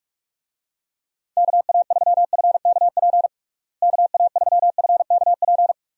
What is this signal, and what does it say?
Morse code, heard as a keyed tone, about 47 wpm, 700 Hz. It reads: KA3FKP KA3FKP
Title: USA Call Signs - 6 Characters - Repeated 3x - 45wpm